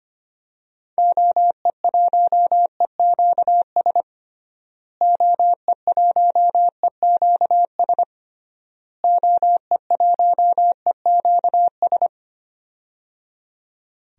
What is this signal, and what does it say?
Morse code, heard as a keyed tone, about 25 wpm, 700 Hz. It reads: OE1EQS OE1EQS OE1EQS
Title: DXCC-100 Call Signs - 6 Characters - Repeated 3x - 25wpm